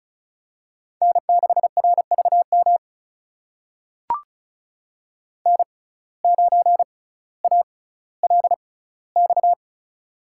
Morse code, heard as a keyed tone, about 34 wpm, spaced as for 18 wpm, 700 Hz. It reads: N6RVM N9ALX
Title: USA Call Signs - 5 Characters - 6x Character Spacing - 35wpm